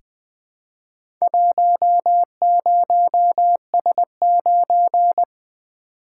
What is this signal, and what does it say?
Morse code, heard as a keyed tone, about 20 wpm, 700 Hz. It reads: 10S9
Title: Four Letter-Number Combinations - 20wpm